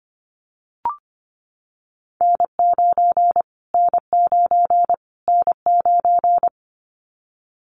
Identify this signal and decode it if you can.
Morse code, heard as a keyed tone, about 25 wpm, 700 Hz. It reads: N9 N9 N9
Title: Callsign Prefixes for Top 25 DXCC - Repeated 3x - 25wpm